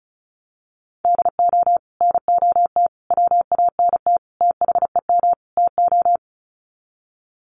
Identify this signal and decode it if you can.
Morse code, heard as a keyed tone, about 34 wpm, 700 Hz. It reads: DO NOT WANT THEM TO
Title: Sets of 5 Words Encore 35wpm